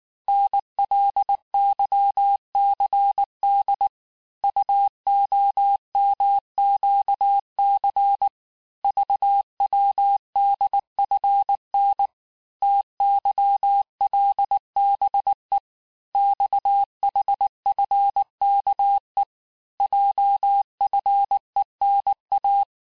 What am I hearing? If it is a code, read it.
Morse code, heard as a keyed tone, about 19 wpm, 800 Hz. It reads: NLYCD UOMQC VWDFN TYLBE XHFKE JFENA
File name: cours-du-lundi files / cours-1583--texte-chiffre-18-mpm